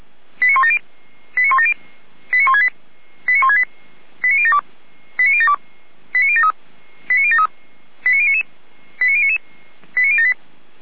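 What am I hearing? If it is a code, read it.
Five-tone selective call (ZVEI), heard as quick runs of five tones, tones of about 70 ms. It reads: 89170 89170 89168 89168 89072 89072 89073 89073 89099 89099 89078
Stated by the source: ZVEI five-tone call 89560 received on an SDR, channel 455.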